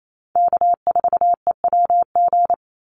Morse code, heard as a keyed tone, about 28 wpm, 700 Hz. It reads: K4EWG
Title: Call Signs Repeated 2x - 28wpm